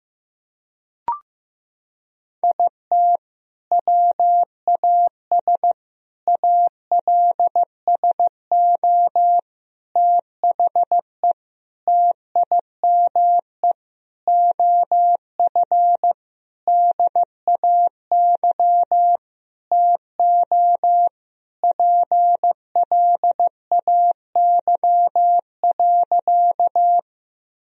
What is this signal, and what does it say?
Morse code, heard as a keyed tone, about 15 wpm, 700 Hz. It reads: IT WAS ALSO THE TIME OF DAY TO PLAY.
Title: Sentences from Top 1000 Words - Review Part 1 - 15wpm